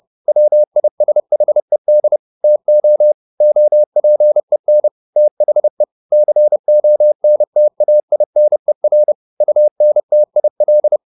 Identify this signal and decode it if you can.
Morse code, heard as a keyed tone, about 30 wpm, 600 Hz. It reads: WISHED TO OPEN THE CONTAINER UNTIL
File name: the old man and the sea / the old man and the sea 1 30wpm